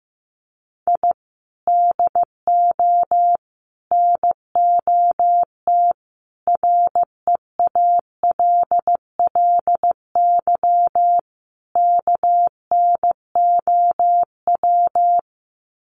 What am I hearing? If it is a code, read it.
Morse code, heard as a keyed tone, about 15 wpm, 700 Hz. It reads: I DO NOT REALLY KNOW
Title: Sets of 5 Words Encore 15wpm